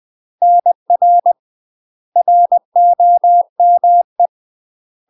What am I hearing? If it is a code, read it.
Morse code, heard as a keyed tone, about 20 wpm, 700 Hz. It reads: NR ROME
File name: UK MASTER QSOs TRAINING / 20WPM / UK MASTER QSOs 0015